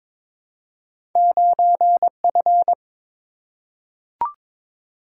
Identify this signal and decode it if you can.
Morse code, heard as a keyed tone, about 21 wpm, 700 Hz. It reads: 9F
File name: practice / 22wpm / Two Letter-Number Combinations - 22wpm